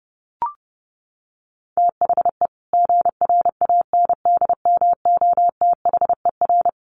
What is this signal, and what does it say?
Morse code, heard as a keyed tone, about 30 wpm, 700 Hz. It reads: THE GRANDMOTHER
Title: Sentences from Top 1200 Words 30wpm